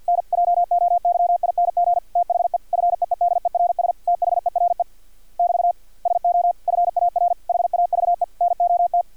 Morse code, heard as a keyed tone, about 50 wpm, 700 Hz. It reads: NJOYING THE FEEDERS THERE <BT> SO FAR HAVE NOT
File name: archived / CRSnet